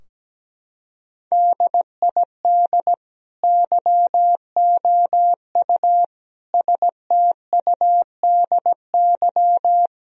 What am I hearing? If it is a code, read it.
Morse code, heard as a keyed tone, about 17 wpm, 700 Hz. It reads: DID YOU STUDY
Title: Sentences from Top 1000 Words - Review Part 3 - 17wpm